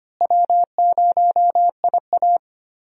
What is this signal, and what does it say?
Morse code, heard as a keyed tone, about 25 wpm, 700 Hz. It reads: W0IA
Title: USA Call Signs - 4 Characters - Repeated 3x - 25wpm